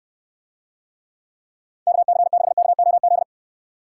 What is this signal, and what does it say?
Morse code, heard as a keyed tone, about 68 wpm, 700 Hz. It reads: KC6KOZ